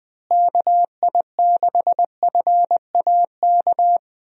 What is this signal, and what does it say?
Morse code, heard as a keyed tone, about 20 wpm, 700 Hz. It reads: KI6FAK
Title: USA Call Signs - 6 Characters - Repeated 3x - 20wpm